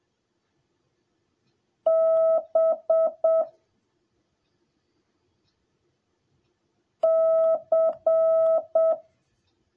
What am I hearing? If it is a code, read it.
Morse code, heard as a keyed tone, about 7 wpm, 650 Hz. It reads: B C
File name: jh8cbh / 2021cw / cw10a